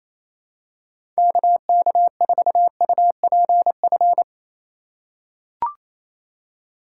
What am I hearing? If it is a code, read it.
Morse code, heard as a keyed tone, about 28 wpm, 700 Hz. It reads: KK4UPF
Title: USA Call Signs - 6 Characters - 3x Character Spacing - 28wpm